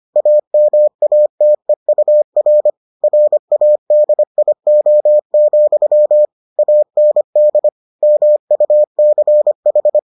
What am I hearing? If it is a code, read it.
Morse code, heard as a keyed tone, about 25 wpm, 600 Hz. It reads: AMATEUR RADIO, AND MUCH